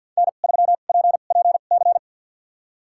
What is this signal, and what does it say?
Morse code, heard as a keyed tone, about 53 wpm, spaced as for 37 wpm, 700 Hz. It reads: N3PPC
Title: USA Call Signs - 5 Characters - 2x Character Spacing - 50wpm